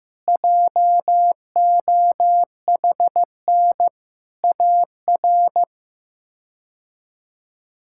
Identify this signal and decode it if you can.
Morse code, heard as a keyed tone, about 15 wpm, 700 Hz. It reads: JOHN AR
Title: K1USN SST QSO Practice - 15wpm